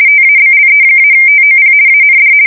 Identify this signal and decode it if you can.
RTTY, about 50 baud. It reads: TABLE-TOP AND
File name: fsk170-50B / teletype-corp